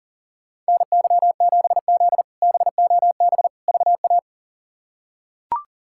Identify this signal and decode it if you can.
Morse code, heard as a keyed tone, about 40 wpm, 700 Hz. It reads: NY7Z BOB VA
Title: K1USN SST QSO Practice - 40wpm